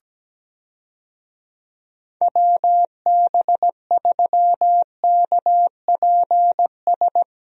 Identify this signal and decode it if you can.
Morse code, heard as a keyed tone, about 17 wpm, 700 Hz. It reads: WB3KPS
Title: USA Call Signs - 6 Characters - Repeated 2x - 17wpm